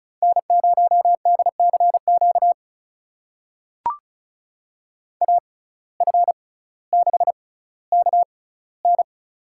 Morse code, heard as a keyed tone, about 36 wpm, spaced as for 18 wpm, 700 Hz. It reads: N0DCQ AF6KN
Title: USA Call Signs - 6x Character Spacing - 35wpm